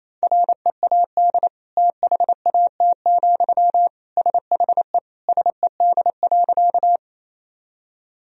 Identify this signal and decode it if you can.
Morse code, heard as a keyed tone, about 28 wpm, 700 Hz. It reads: READ THAT, SHE SED.
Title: Sentences from Top 200 Words and Abbreviations 28wpm